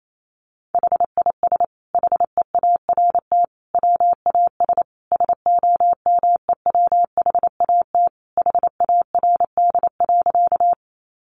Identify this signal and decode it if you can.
Morse code, heard as a keyed tone, about 28 wpm, 700 Hz. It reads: HIS HEART WAS SOMEWHAT HARD.
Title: Sentences from Top 1300 Words 28wpm